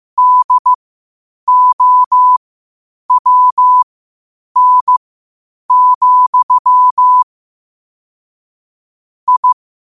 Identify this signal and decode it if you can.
Morse code, heard as a keyed tone, about 15 wpm, spaced as for 8 wpm, 1000 Hz. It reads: DOWN, I
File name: qotdfiles / 8wpm